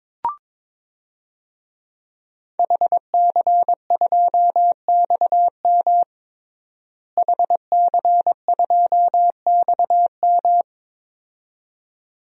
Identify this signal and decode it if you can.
Morse code, heard as a keyed tone, about 22 wpm, 700 Hz. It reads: HC2XM HC2XM
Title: DXCC-100 Call Signs - 5 Characters - Repeated 2x - 22wpm